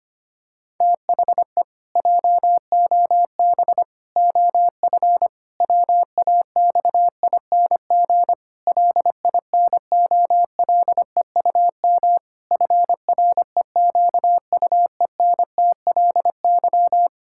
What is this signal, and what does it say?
Morse code, heard as a keyed tone, about 25 wpm, 700 Hz. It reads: THE JOB OF WAXING LINOLEUM FREQUENTLY
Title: Pangram Challenge v2 - 25wpm